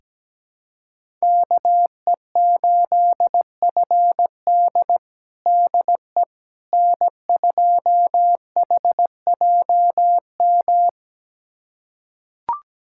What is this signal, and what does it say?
Morse code, heard as a keyed tone, about 17 wpm, 700 Hz. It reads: KE8FD DE N2HJM
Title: USA Callsign DE Callsign - 5 Characters - 17wpm